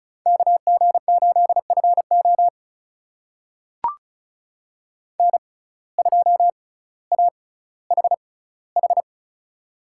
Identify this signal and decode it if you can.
Morse code, heard as a keyed tone, about 35 wpm, spaced as for 18 wpm, 700 Hz. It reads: KG8FO N2AHH
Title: USA Call Signs - 5 Characters - 6x Character Spacing - 35wpm